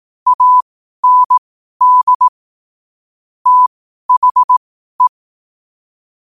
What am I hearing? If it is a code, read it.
Morse code, heard as a keyed tone, about 18 wpm, spaced as for 12 wpm, 1000 Hz. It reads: AND THE
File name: qotdfiles / 12wpm